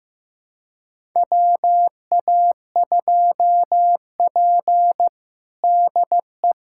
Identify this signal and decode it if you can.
Morse code, heard as a keyed tone, about 15 wpm, 700 Hz. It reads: WA2P DE
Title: USA Callsign DE Callsign - 4 Characters - 15wpm